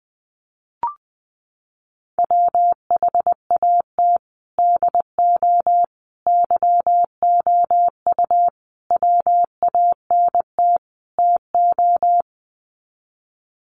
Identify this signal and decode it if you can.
Morse code, heard as a keyed tone, about 20 wpm, 700 Hz. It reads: WHAT DO YOU WANT TO
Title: Sets of 5 Words 20wpm